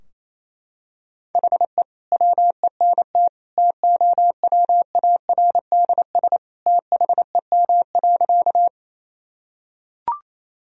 Morse code, heard as a keyed tone, about 27 wpm, 700 Hz. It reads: HE WENT TOWARDS THEM.
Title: Sentences from Top 1000 Words - Review Part 2 - 28wpm